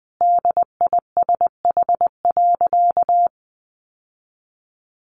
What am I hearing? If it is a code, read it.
Morse code, heard as a keyed tone, about 20 wpm, 700 Hz. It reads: DISH.
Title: Sentences from Top 1300 Words 20wpm